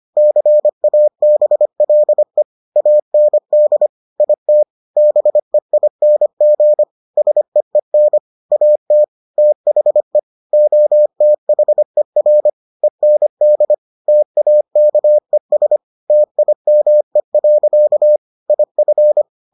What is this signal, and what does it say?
Morse code, heard as a keyed tone, about 25 wpm, 600 Hz. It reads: CABLE AND IT BEING SEEN AT THE OTHER END TAKES TIME. IF